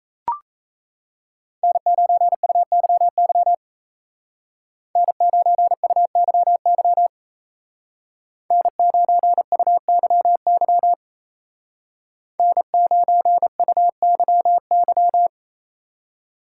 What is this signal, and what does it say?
Morse code, heard as a keyed tone, about 33 wpm, 700 Hz. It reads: N9UYY N9UYY N9UYY N9UYY
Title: Speed Racing - USA 1x3 Call Signs 28wpm